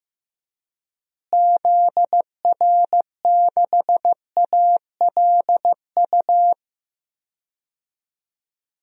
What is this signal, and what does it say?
Morse code, heard as a keyed tone, about 15 wpm, 700 Hz. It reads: ZR6ALU
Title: DXCC-100 Call Signs - 6 Characters - Repeated 3x - 15wpm